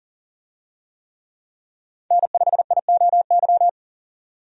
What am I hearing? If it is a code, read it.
Morse code, heard as a keyed tone, about 40 wpm, 700 Hz. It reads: N5IOY
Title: USA Call Signs - 5 Characters - 40wpm